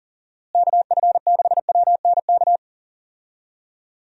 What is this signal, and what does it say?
Morse code, heard as a keyed tone, about 40 wpm, 700 Hz. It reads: KF6WNK
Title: USA Call Signs - 6 Characters - Repeated 3x - 40wpm